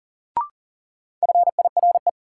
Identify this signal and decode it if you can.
Morse code, heard as a keyed tone, about 40 wpm, 700 Hz. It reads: FIRE
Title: Sentences from Top 400 Words - New Word Drill - 40wpm